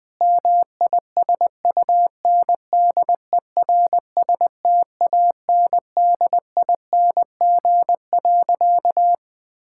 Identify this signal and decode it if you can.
Morse code, heard as a keyed tone, about 20 wpm, 700 Hz. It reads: MISUNDERSTANDING.
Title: Sentences with 4-Word Phrases - 20wpm